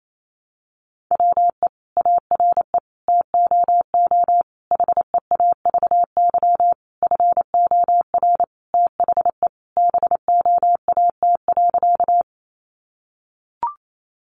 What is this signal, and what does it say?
Morse code, heard as a keyed tone, about 28 wpm, 700 Hz. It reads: WE ARE TOO HEAVY FOR THE BOAT.
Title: Sentences from Top 1100 Words 28wpm